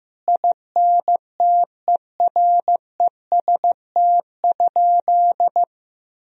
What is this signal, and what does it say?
Morse code, heard as a keyed tone, about 16 wpm, 700 Hz. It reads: INTEREST?
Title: Sentences from Top 1000 Words - Review Part 2 - 15wpm